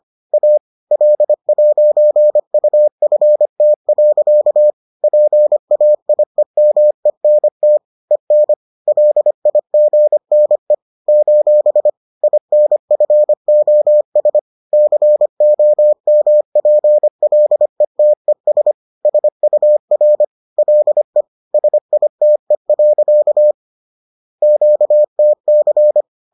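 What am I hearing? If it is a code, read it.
Morse code, heard as a keyed tone, about 25 wpm, 600 Hz. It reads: A L'UFT. PAIEMENT EN LIGNE : INFOS COMPLETES SUR LE SITE. QTC
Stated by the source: Bulletin sent in CW.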